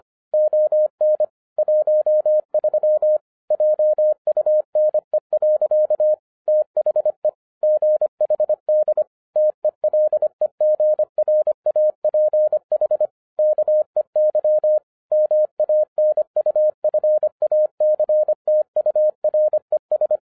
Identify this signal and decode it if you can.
Morse code, heard as a keyed tone, about 25 wpm, 610 Hz. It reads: ON 13 JUNE. THE GHD TELEGRAPH KEY MANUFACTURES